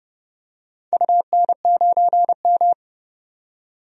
Morse code, heard as a keyed tone, about 30 wpm, 700 Hz. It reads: UN9M
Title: DXCC-100 Call Signs - 4 Characters - 30wpm